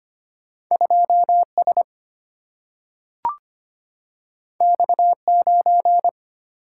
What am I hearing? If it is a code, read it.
Morse code, heard as a keyed tone, about 25 wpm, 700 Hz. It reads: 2 S X 9
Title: Single-Letter Single-Number 25wpm